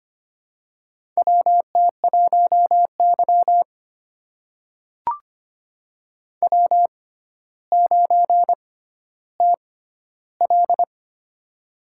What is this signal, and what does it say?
Morse code, heard as a keyed tone, about 25 wpm, spaced as for 13 wpm, 700 Hz. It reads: WT1Y W9TL
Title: USA Call Signs - 4 Characters - 6x Character Spacing - 25wpm